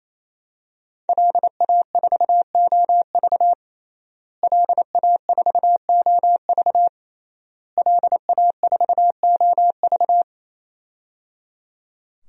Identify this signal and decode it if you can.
Morse code, heard as a keyed tone, about 27 wpm, 700 Hz. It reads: LA4OV LA4OV LA4OV
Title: DXCC-100 Call Signs - 5 Characters - Repeated 3x - 28wpm